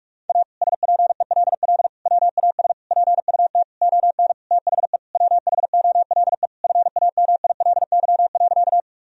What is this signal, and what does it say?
Morse code, heard as a keyed tone, about 46 wpm, 700 Hz. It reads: A SPELL WAS PUT ON THE WHOLE FAMILY.
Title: Sentences from Top 1000 Words - Review Part 1 - 45wpm